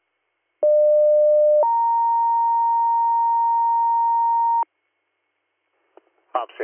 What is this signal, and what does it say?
Two-tone page, heard as a one-second tone, then a three-second tone, about 600 then 930 Hz, with speech. It is heard from 0.6 to 4.6 s.